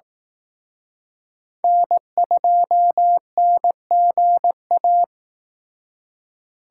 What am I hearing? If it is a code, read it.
Morse code, heard as a keyed tone, about 18 wpm, 700 Hz. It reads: N2NGA
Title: Speed Racing - USA 1x3 Call Signs 15wpm